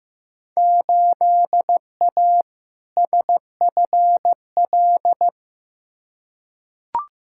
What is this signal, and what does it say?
Morse code, heard as a keyed tone, about 15 wpm, 700 Hz. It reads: 8A SFL
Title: Field Day Exchanges Encore - 15wpm